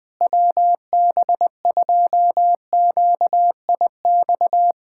Morse code, heard as a keyed tone, about 20 wpm, 700 Hz. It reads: WB2QIX
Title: Speed-Racing Common QSO Phrases - 20wpm